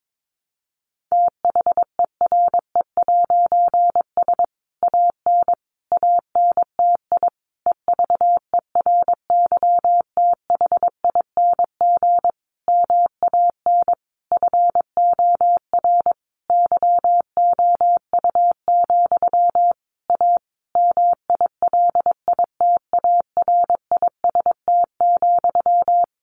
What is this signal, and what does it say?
Morse code, heard as a keyed tone, about 22 wpm, 700 Hz. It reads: THERE'S AN ANTI EVERYTHING MAN FOR YOU, A MILITARIST,
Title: A Sound Of Thunder 22wpm